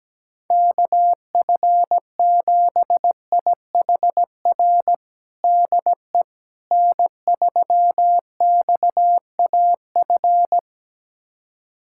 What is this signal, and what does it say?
Morse code, heard as a keyed tone, about 17 wpm, 700 Hz. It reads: KF7IHR DE N3XAF
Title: USA Callsign DE Callsign 17wpm